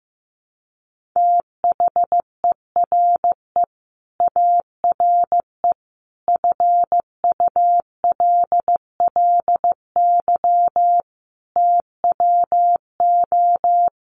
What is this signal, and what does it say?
Morse code, heard as a keyed tone, about 15 wpm, 700 Hz. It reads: THERE ARE FULLY TWO
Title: Sentences from Top 1300 Words 15wpm